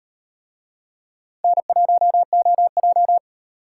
Morse code, heard as a keyed tone, about 38 wpm, 700 Hz. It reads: N1OJ